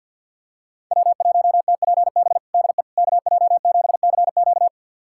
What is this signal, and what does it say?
Morse code, heard as a keyed tone, about 52 wpm, 700 Hz. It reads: W1TPB DE KJ7CQ
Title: USA Callsign DE Callsign - 5 Characters - 50wpm